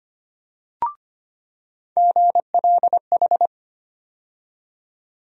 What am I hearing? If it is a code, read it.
Morse code, heard as a keyed tone, about 25 wpm, 700 Hz. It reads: GLH
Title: USA Call Sign Postfixes - 3 Letters 25wpm